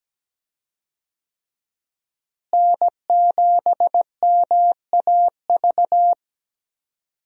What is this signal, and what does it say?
Morse code, heard as a keyed tone, about 17 wpm, 700 Hz. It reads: N7MAV
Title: USA Call Signs - 5 Characters - Repeated 3x - 17wpm